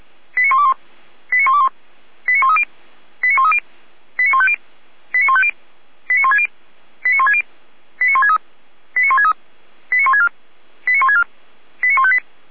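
Five-tone selective call (ZVEI), heard as quick runs of five tones, tones of about 80 ms. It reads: 89121 89121 89130 89130 89150 89150 89160 89160 89163 89163 89164 89164 89158
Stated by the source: ZVEI five-tone call received on SDR, from the feuerwehr-orlamuende.org fire brigade (f85360) audio server.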